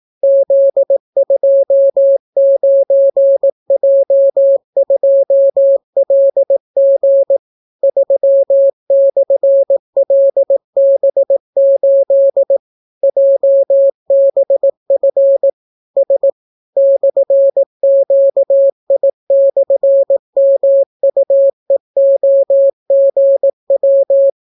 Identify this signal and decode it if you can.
Morse code, heard as a keyed tone, about 18 wpm, 550 Hz. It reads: Z29J2LG 3/LB8 JBF S /QI/MUEOGW